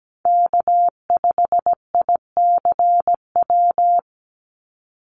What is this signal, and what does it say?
Morse code, heard as a keyed tone, about 17 wpm, 700 Hz. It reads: K5ICW
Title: USA Call Signs 17wpm